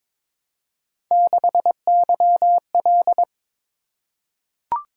Morse code, heard as a keyed tone, about 22 wpm, 700 Hz. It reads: 6YL
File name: practice / 22wpm / Three Letter-Number Combinations - 22wpm